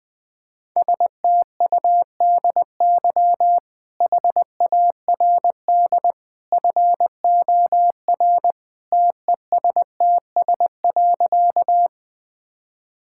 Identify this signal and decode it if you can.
Morse code, heard as a keyed tone, about 20 wpm, 700 Hz. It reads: STUDY HARD FOR TESTS.
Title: Easy Sentences with Hints - 20wpm